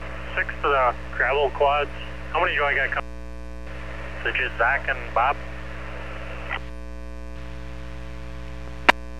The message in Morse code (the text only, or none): none